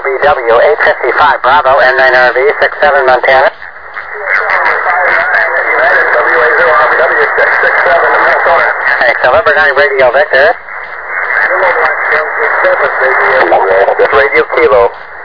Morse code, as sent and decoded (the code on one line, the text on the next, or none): none